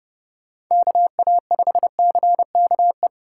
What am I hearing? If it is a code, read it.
Morse code, heard as a keyed tone, about 29 wpm, 700 Hz. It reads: KA5CKE